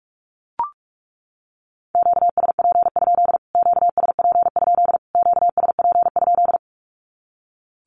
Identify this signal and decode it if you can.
Morse code, heard as a keyed tone, about 44 wpm, 700 Hz. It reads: QSP? QSP? QSP?
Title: Q-Codes Repeated 3x 45wpm